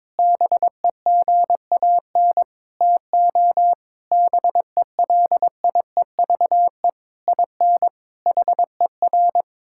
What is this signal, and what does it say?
Morse code, heard as a keyed tone, about 22 wpm, 700 Hz. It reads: BEGAN TO BELIEVE IN HER
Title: Sentences from Top 1600 Words 22wpm